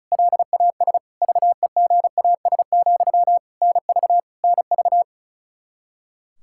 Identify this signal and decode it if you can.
Morse code, heard as a keyed tone, about 34 wpm, 700 Hz. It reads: LAS VEGAS, NV NV
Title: K1USN SST QSO Practice 35wpm